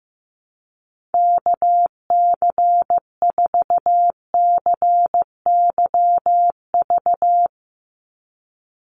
Morse code, heard as a keyed tone, about 15 wpm, 700 Hz. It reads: KC4CYV
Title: USA Call Signs 15wpm